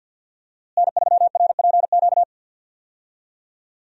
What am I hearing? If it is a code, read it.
Morse code, heard as a keyed tone, about 50 wpm, 700 Hz. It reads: N3RPQ
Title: USA Call Signs - 5 Characters - Repeated 3x - 50wpm